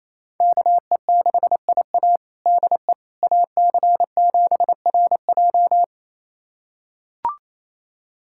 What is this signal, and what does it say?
Morse code, heard as a keyed tone, about 28 wpm, 700 Hz. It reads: KE6IA DE AC7RJ
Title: USA Callsign DE Callsign - 5 Characters - 28wpm